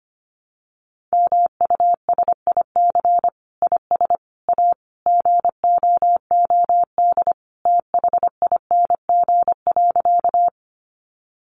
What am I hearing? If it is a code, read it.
Morse code, heard as a keyed tone, about 25 wpm, 700 Hz. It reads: MUSIC IS A GOOD THING.
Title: Sentences from Top 300 Words 25wpm